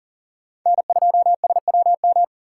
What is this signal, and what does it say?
Morse code, heard as a keyed tone, about 40 wpm, 700 Hz. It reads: N2SWM
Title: USA Call Signs - 5 Characters - 4x Character Spacing - 40wpm